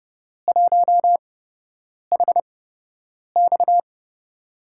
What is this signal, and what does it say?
Morse code, heard as a keyed tone, about 29 wpm, spaced as for 8 wpm, 700 Hz. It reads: 1HX
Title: USA Call Signs - 8x Character Spacing - 30wpm